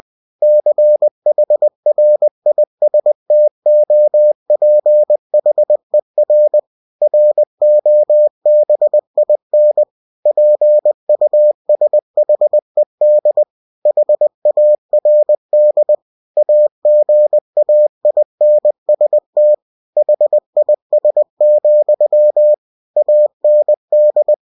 Morse code, heard as a keyed tone, about 20 wpm, 600 Hz. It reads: CHRISTOPHER ROBIN PUSHED HARD AGAINST HIS, AND
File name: cw-books / winnie-the-pooh / winnie-20wpm-page-0076